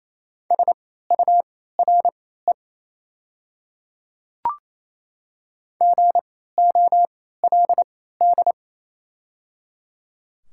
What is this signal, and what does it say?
Morse code, heard as a keyed tone, about 27 wpm, spaced as for 16 wpm, 690 Hz. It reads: SURE GOLD
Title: 3x Character Spacing - Top 500 Words - 28wpm